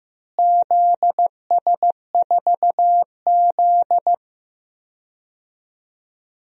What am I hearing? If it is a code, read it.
Morse code, heard as a keyed tone, about 15 wpm, 700 Hz. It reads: ZS4Z
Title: DXCC-100 Call Signs - 4 Characters - Repeated 2x - 15wpm